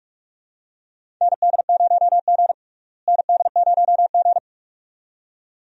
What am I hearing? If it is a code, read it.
Morse code, heard as a keyed tone, about 42 wpm, 700 Hz. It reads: ND0G ND0G
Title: USA Call Signs - 4 Characters - Repeated 2x - 45wpm